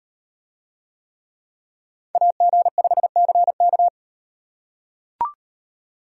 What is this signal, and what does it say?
Morse code, heard as a keyed tone, about 38 wpm, 700 Hz. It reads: AG5CK